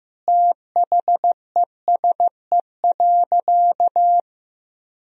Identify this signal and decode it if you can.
Morse code, heard as a keyed tone, about 15 wpm, 700 Hz. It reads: THESE.
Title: Sentences from Top 1000 Words - Review Part 1 - 15wpm